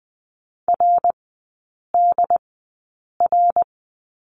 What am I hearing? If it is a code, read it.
Morse code, heard as a keyed tone, about 20 wpm, spaced as for 8 wpm, 700 Hz. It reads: RDR
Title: 184 Lesson 30e - Letter b - Three Characters 20wpm